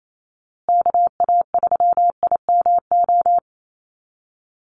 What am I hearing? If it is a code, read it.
Morse code, heard as a keyed tone, about 28 wpm, 700 Hz. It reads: KA3IMO
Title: USA Call Signs Repeated 3x 28wpm